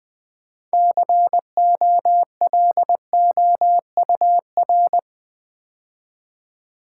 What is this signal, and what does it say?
Morse code, heard as a keyed tone, about 20 wpm, 700 Hz. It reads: COLOUR
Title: Build A Word - 2 Letters At A Time - 8-Letter Words - 20wpm